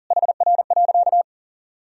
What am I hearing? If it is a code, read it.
Morse code, heard as a keyed tone, about 40 wpm, 700 Hz. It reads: HR.
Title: Sentences from Top 500 Words and Abbreviations 40wpm